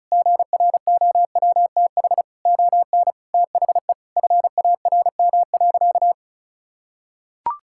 Morse code, heard as a keyed tone, about 36 wpm, 700 Hz. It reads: GROWTH ON THE FARM.
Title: Sentences from Top 1000 Words - Review Part 5 - 35wpm